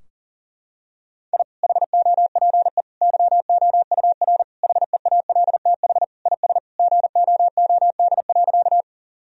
Morse code, heard as a keyed tone, about 40 wpm, 700 Hz. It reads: I HOPE YOUR HEALTH IS GOOD.
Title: Sentences from Top 1000 Words - Review Part 3 - 40wpm